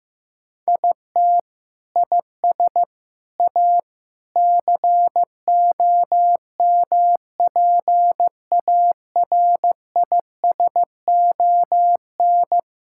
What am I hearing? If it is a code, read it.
Morse code, heard as a keyed tone, about 16 wpm, 700 Hz. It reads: IT IS A COMPARISON
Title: Sentences from Top 1600 Words 15wpm